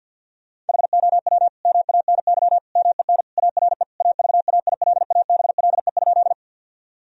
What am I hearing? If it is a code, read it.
Morse code, heard as a keyed tone, about 46 wpm, 700 Hz. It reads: HOW MANY MEN ARE AVAILABLE?
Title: Sentences from Top 1000 Words - Review Part 3 - 50wpm